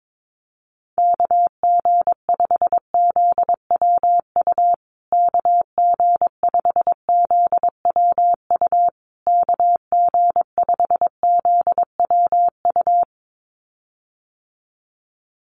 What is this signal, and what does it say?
Morse code, heard as a keyed tone, about 22 wpm, 700 Hz. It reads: KG5ZWU KG5ZWU KG5ZWU
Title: USA Call Signs Repeated 3x 22wpm